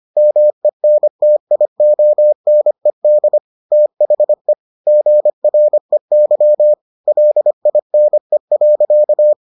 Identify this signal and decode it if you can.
Morse code, heard as a keyed tone, about 25 wpm, 600 Hz. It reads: MENTIONED THE GREY LINE.